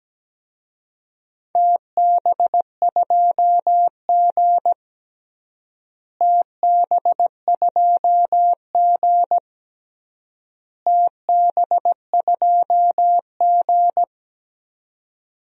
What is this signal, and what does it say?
Morse code, heard as a keyed tone, about 17 wpm, 700 Hz. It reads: TB2G TB2G TB2G
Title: DXCC-100 Call Signs - 4 Characters - Repeated 3x - 17wpm